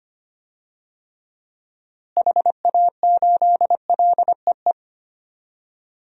Morse code, heard as a keyed tone, about 25 wpm, 700 Hz. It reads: HA8LEE
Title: DXCC-100 Call Signs - 6 Characters - Repeated 2x - 25wpm